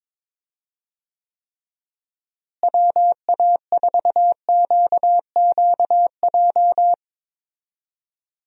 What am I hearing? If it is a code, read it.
Morse code, heard as a keyed tone, about 22 wpm, 700 Hz. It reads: WA4QQJ
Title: USA Call Signs - 6 Characters - Repeated 2x - 22wpm